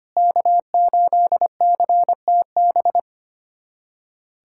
Morse code, heard as a keyed tone, about 25 wpm, 700 Hz. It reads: K8CTB
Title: USA Callsign DE Callsign - 5 Characters - 25wpm